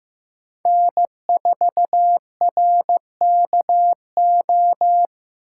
Morse code, heard as a keyed tone, about 15 wpm, 700 Hz. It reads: N4RKO
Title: USA Call Signs - 5 Characters - 6x Character Spacing - 15wpm